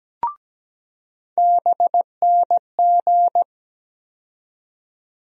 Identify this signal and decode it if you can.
Morse code, heard as a keyed tone, about 17 wpm, 700 Hz. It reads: BNG
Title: USA Call Sign Postfixes - 3 Letters 17wpm